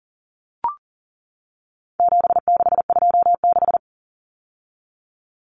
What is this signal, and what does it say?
Morse code, heard as a keyed tone, about 40 wpm, 700 Hz. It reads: 7626
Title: Numbers - 4-Digit 40wpm